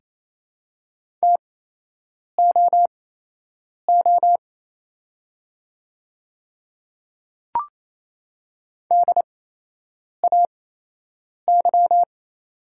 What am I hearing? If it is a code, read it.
Morse code, heard as a keyed tone, about 28 wpm, spaced as for 8 wpm, 700 Hz. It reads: TOO DAY
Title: 8x Character Spacing - Top 200 Words - 28wpm